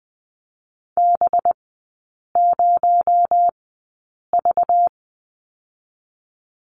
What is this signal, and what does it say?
Morse code, heard as a keyed tone, about 20 wpm, 700 Hz. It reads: B 0 V
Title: 218 Lesson 35e - Number 0 - Three Characters 20wpm